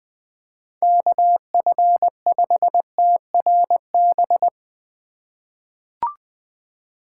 Morse code, heard as a keyed tone, about 21 wpm, 700 Hz. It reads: KF5TRB
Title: USA Call Signs - 6 Characters - 2x Character Spacing - 20wpm